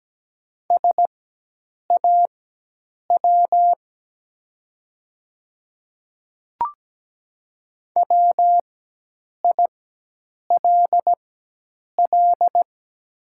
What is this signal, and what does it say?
Morse code, heard as a keyed tone, about 17 wpm, spaced as for 8 wpm, 700 Hz. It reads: SAW WILL
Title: 4x Character Spacing - Top 500 Words - 17wpm